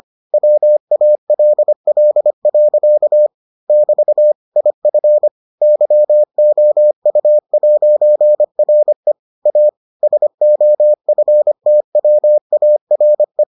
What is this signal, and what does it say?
Morse code, heard as a keyed tone, about 25 wpm, 600 Hz. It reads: WALL. <BT> IF YOU'RE A SOFTWARE